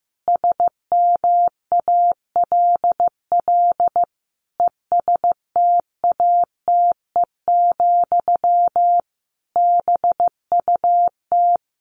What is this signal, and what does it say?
Morse code, heard as a keyed tone, about 15 wpm, 700 Hz. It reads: SMALL ESTATE, BUT
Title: Sentences from Top 1200 Words 15wpm